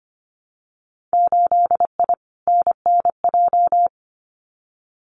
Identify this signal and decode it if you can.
Morse code, heard as a keyed tone, about 25 wpm, 700 Hz. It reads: 8I NNJ
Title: Winter Field Day Exchanges Encore 25wpm